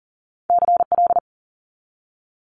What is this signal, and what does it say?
Morse code, heard as a keyed tone, about 39 wpm, 700 Hz. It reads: CL
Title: Multi-letter Phonemes 40wpm